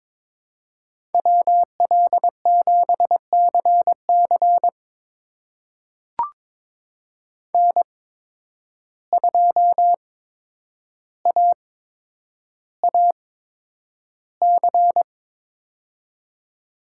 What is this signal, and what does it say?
Morse code, heard as a keyed tone, about 22 wpm, spaced as for 9 wpm, 700 Hz. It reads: WL7CC N2AAC
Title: USA Call Signs - 5 Characters - 8x Character Spacing - 22wpm